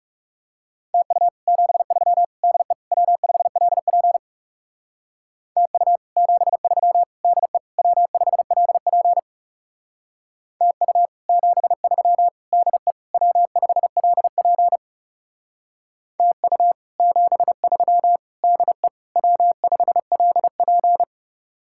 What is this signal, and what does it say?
Morse code, heard as a keyed tone, about 36 wpm, 700 Hz. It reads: TU 73 DE W5LP TU 73 DE W5LP TU 73 DE W5LP TU 73 DE W5LP
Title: Speed-Racing Common QSO Phrases - 30wpm